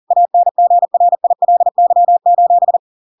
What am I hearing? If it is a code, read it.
Morse code, heard as a keyed tone, about 40 wpm, 700 Hz. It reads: ANGRILY:
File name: Pinocchio / Pinocchio ch6 40 40 ik1hge